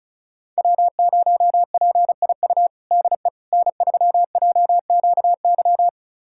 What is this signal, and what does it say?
Morse code, heard as a keyed tone, about 34 wpm, 700 Hz. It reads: W0PIU DE N3JQY
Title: USA Callsign DE Callsign - 5 Characters - 35wpm